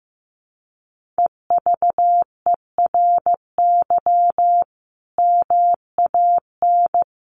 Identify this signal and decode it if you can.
Morse code, heard as a keyed tone, about 15 wpm, 700 Hz. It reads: EVERY MAN
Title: Sentences from Top 1500 Words 15wpm